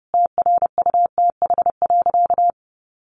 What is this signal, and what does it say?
Morse code, heard as a keyed tone, about 30 wpm, 700 Hz. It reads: TRUTH.
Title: Sentences from Top 1500 Words 30wpm